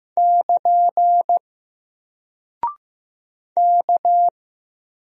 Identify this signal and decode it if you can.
Morse code, heard as a keyed tone, about 15 wpm, 700 Hz. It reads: <KN> K
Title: Single Character (Letter:Number:Punctuation:Prosign) Alt 15wpm